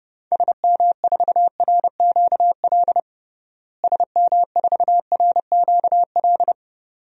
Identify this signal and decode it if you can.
Morse code, heard as a keyed tone, about 30 wpm, 700 Hz. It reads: SM4RQL SM4RQL
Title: DXCC-100 Call Signs - 6 Characters - Repeated 2x - 30wpm